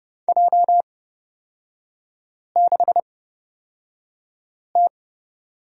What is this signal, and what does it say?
Morse code, heard as a keyed tone, about 30 wpm, spaced as for 5 wpm, 700 Hz. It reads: J6T
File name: Lesson 36e - BT - Three Characters 30wpm